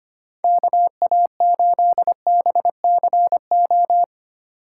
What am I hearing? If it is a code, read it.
Morse code, heard as a keyed tone, about 25 wpm, 700 Hz. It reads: KA8BCO
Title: USA Call Signs - 6 Characters - 8x Character Spacing - 25wpm